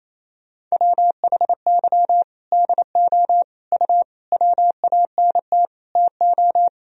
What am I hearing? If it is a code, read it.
Morse code, heard as a keyed tone, about 28 wpm, 700 Hz. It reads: WHY DO U WANT TO
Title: Sentences from Top 200 Words and Abbreviations 28wpm